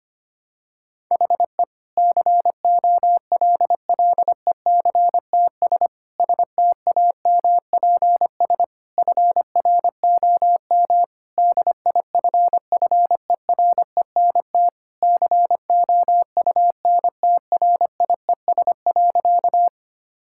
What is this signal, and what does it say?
Morse code, heard as a keyed tone, about 25 wpm, 700 Hz. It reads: HE COLLECTS STAMPS FROM DIFFERENT COUNTRIES.
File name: practice / 25wpm / Difficult Sentences with Hints - 25wpm